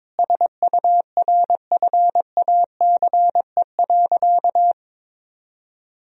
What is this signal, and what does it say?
Morse code, heard as a keyed tone, about 22 wpm, 700 Hz. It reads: SURFACE.
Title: Sentences from Top 1000 Words - Review Part 3 - 22wpm